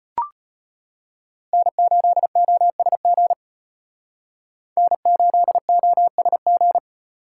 Morse code, heard as a keyed tone, about 38 wpm, 700 Hz. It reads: N8OSG N8OSG